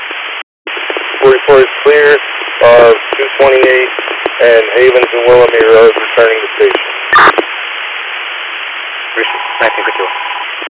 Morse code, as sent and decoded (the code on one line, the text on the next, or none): none